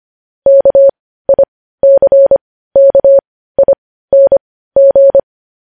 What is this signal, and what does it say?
Morse code, heard as a keyed tone, about 25 wpm, spaced as for 15 wpm, 550 Hz. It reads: KICKING
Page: news-headlines-in-morse-code-at-15-wpm